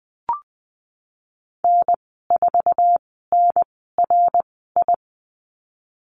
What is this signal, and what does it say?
Morse code, heard as a keyed tone, about 20 wpm, spaced as for 15 wpm, 700 Hz. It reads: N4NRI
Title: 062 Lesson 11g - Letter h - Call Signs 20wpm